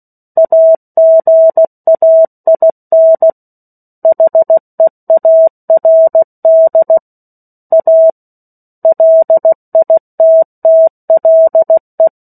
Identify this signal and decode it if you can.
Morse code, heard as a keyed tone, about 16 wpm, 650 Hz. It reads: AGAIN HEARD A LITTLE
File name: 03-Chapter-03 16 WPM